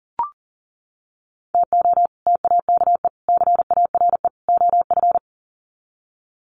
Morse code, heard as a keyed tone, about 40 wpm, 700 Hz. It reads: TO TAKE CARE OF
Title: Sets of 4 Words 40wpm